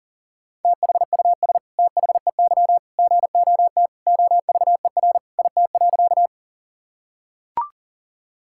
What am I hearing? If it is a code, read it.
Morse code, heard as a keyed tone, about 40 wpm, 700 Hz. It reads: THUS THEY GOT OVER IT.
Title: Sentences from Top 1000 Words - Review Part 5 - 40wpm